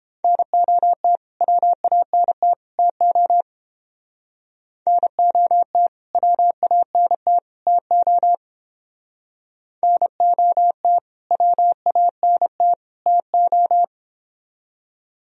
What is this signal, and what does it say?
Morse code, heard as a keyed tone, about 29 wpm, 700 Hz. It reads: NOT WANT TO NOT WANT TO NOT WANT TO